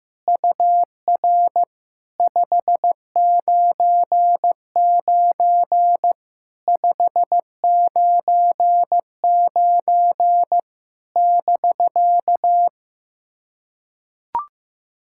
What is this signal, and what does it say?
Morse code, heard as a keyed tone, about 15 wpm, 700 Hz. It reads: UR 599 599 <BK>
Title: SOTA QSO Practice 15wpm